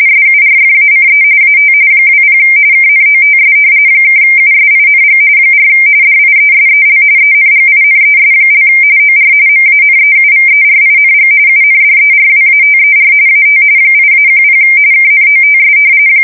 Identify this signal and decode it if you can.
RTTY, about 45.45 baud. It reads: SURFACE AT 20 KNOTS (USING 4 DIESEL ENGINES) OR MANEUVERING UNDERWATER AT 8-10 KNOTS
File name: fsk170 / usnavy4